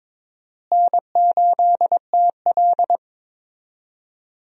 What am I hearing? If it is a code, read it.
Morse code, heard as a keyed tone, about 22 wpm, 700 Hz. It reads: N8TL
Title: USA Call Signs - 4 Characters - Repeated 3x - 22wpm